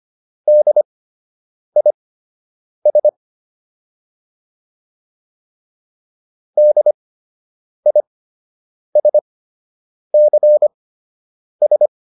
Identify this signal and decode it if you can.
Morse code, heard as a keyed tone, about 25 wpm, spaced as for 8 wpm, 600 Hz. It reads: DIS DISCS